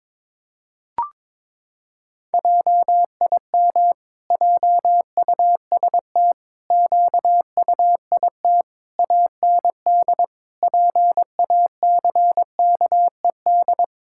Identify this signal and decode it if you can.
Morse code, heard as a keyed tone, about 22 wpm, 700 Hz. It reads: JIM JUST QUIT AND PACKED
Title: Pangram Challenge v2 - 22wpm